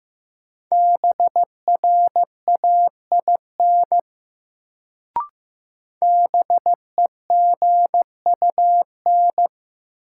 Morse code, heard as a keyed tone, about 15 wpm, 700 Hz. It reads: BRAIN BEGUN
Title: Sentences from Top 1000 Words - Review Part 6 - New Word Drill - 15wpm